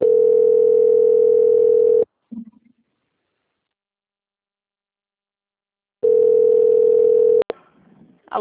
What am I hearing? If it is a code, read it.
Ringback tone (North America), 440 + 480 Hz.
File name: out- - - - - .26017